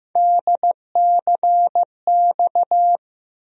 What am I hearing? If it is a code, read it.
Morse code, heard as a keyed tone, about 15 wpm, 700 Hz. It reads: DCX
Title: USA Call Sign Postfixes - 3 Letters 15wpm